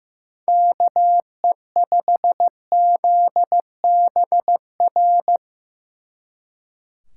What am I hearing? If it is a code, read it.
Morse code, heard as a keyed tone, about 17 wpm, 700 Hz. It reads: KE5ZBR
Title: Speed Racing - USA 2x3 Call Signs 15wpm